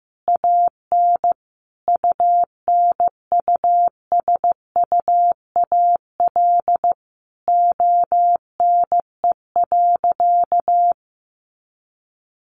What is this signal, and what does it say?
Morse code, heard as a keyed tone, about 15 wpm, 700 Hz. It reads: AN UNUSUAL ONE.
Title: Sentences from Top 1400 Words 15wpm